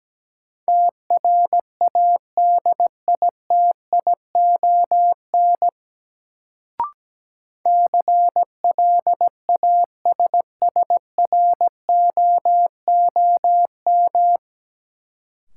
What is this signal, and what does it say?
Morse code, heard as a keyed tone, about 17 wpm, 700 Hz. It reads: TRADITION CLASSROOM
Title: Sentences from Top 1500 Words - New Word Drill - 17wpm